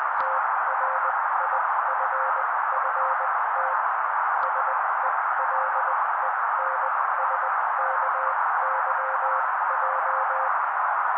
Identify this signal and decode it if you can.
Morse code, heard as a keyed tone, about 20 wpm, spaced as for 16 wpm, 530 Hz. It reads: TRIFFT SELENSKYJ
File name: daily / 16wpm noise